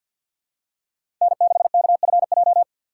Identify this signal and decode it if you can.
Morse code, heard as a keyed tone, about 51 wpm, 700 Hz. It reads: N6KFJ
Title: USA Call Signs - 3x Character Spacing - 50wpm